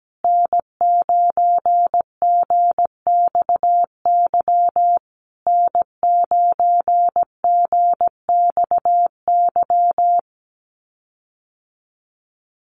Morse code, heard as a keyed tone, about 17 wpm, 700 Hz. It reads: N9GXY N9GXY
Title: USA Call Signs Repeated 3x 17wpm